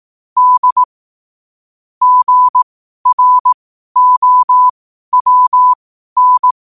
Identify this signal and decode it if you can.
Morse code, heard as a keyed tone, about 18 wpm, spaced as for 12 wpm, 1000 Hz. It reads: D GROWN
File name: qotdfiles / 12wpm